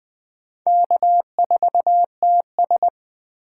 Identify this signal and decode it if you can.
Morse code, heard as a keyed tone, about 20 wpm, 700 Hz. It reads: K4TS
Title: USA Call Signs - 4 Characters - 8x Character Spacing - 20wpm